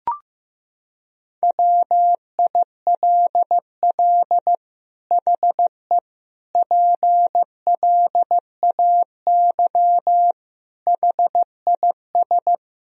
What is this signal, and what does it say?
Morse code, heard as a keyed tone, about 15 wpm, 700 Hz. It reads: WILL HE PLAY HIS
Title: Sentences from Top 1000 Words - Review Part 1 - 15wpm